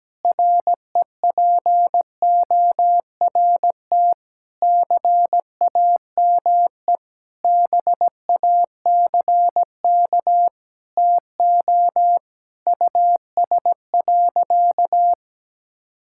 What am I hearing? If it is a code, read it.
Morse code, heard as a keyed tone, about 17 wpm, 700 Hz. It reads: REPORT CAME BACK TO US.
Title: Sentences from Top 1000 Words - Review Part 6 - 17wpm